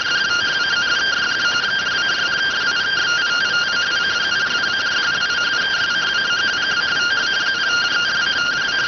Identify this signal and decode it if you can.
Two-tone teleprinter signal, about 170 Hz shift.